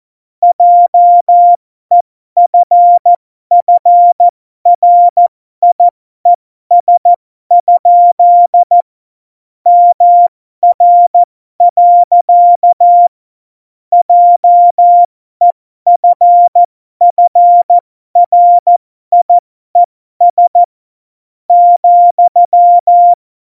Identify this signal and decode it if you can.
Morse code, heard as a keyed tone, about 14 wpm, spaced as for 12 wpm, 700 Hz. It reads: JEFFRIES? MR. JEFFRIES ,
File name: Morsekurs / Witze